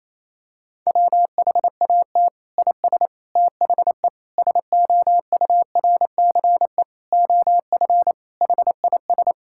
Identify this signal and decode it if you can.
Morse code, heard as a keyed tone, about 28 wpm, 700 Hz. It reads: WHAT IS THE SOURCE OF HIS
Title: Sentences from Top 1000 Words - Review Part 5 - 28wpm